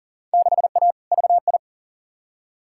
Morse code, heard as a keyed tone, about 40 wpm, 700 Hz. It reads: 6A VI
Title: Field Day Exchanges Encore - 40wpm